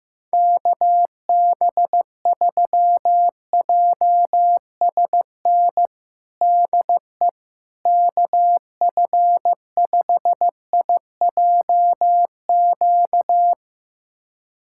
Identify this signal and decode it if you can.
Morse code, heard as a keyed tone, about 16 wpm, 700 Hz. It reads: KB3JSN DE KF5IJQ
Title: USA Callsign DE Callsign - 6 Characters - 15wpm